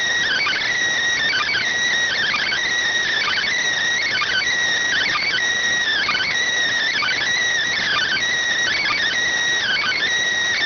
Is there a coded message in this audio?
Yes: Emergency Alert System (SAME) tones (header bursts) from 0.2 to 10.0 s, heard as harsh buzzing bursts.